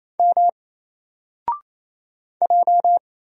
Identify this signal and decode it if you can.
Morse code, heard as a keyed tone, about 27 wpm, 700 Hz. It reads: M J
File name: Single-Letter-Number-28wpm